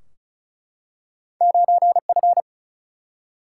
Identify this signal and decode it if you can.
Morse code, heard as a keyed tone, about 36 wpm, 700 Hz. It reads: 9 F
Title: Single-Letter Single-Number 35wpm